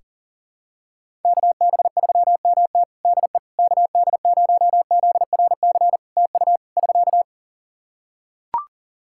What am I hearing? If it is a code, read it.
Morse code, heard as a keyed tone, about 40 wpm, 700 Hz. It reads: KB3MT DE KD0ZRC TU <SK>
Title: Practice QSO - Exchange 4 - Ending - 40wpm